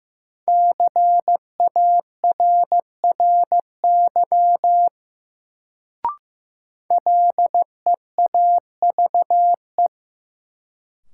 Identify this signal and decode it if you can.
Morse code, heard as a keyed tone, about 15 wpm, 700 Hz. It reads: CARRY LEAVE
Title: Sentences from Top 400 Words - New Word Drill - 15wpm